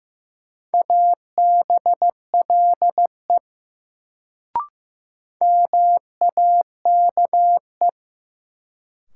Morse code, heard as a keyed tone, about 16 wpm, 700 Hz. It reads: ABLE MAKE
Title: Sets of 2 Words - New Word Drill - 15wpm